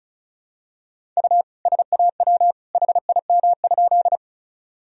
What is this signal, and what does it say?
Morse code, heard as a keyed tone, about 36 wpm, 700 Hz. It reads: U SAW HIM?
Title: Sentences from Top 300 Words and Abbreviations 35wpm